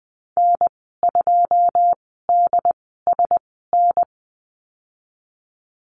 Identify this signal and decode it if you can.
Morse code, heard as a keyed tone, about 20 wpm, spaced as for 15 wpm, 700 Hz. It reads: N2DSN
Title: 154 Lesson 25g - Letter p - Call Signs 20wpm